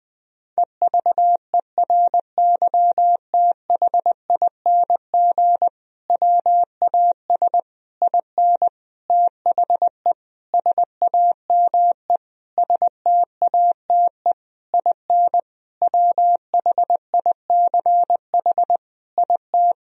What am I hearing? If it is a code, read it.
Morse code, heard as a keyed tone, about 20 wpm, 700 Hz. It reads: EVERYTHING WAS IN THE SAME STATE IN WHICH IT
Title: Sentences from Top 1000 Words - Review Part 2 - 20wpm